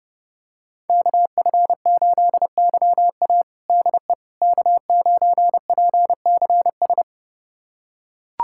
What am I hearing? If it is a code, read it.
Morse code, heard as a keyed tone, about 30 wpm, 700 Hz. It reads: KF8YA DE K9PCS
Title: USA Callsign DE Callsign - 5 Characters - 30wpm